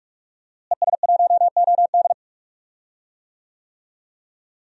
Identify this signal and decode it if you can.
Morse code, heard as a keyed tone, about 46 wpm, 700 Hz. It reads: ES1OD
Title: DXCC-100 Call Signs - 5 Characters - Repeated 2x - 45wpm